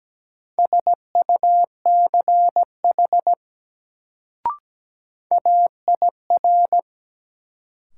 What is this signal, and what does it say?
Morse code, heard as a keyed tone, about 17 wpm, 700 Hz. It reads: SUCH AIR
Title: Sentences from Top 200 Words - New Word Drill - 17wpm